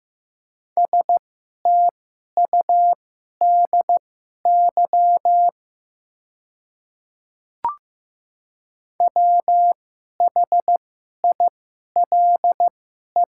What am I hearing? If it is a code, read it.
Morse code, heard as a keyed tone, about 15 wpm, spaced as for 11 wpm, 700 Hz. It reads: STUDY WHILE
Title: 2x Character Spacing - Top 300 Words - 15wpm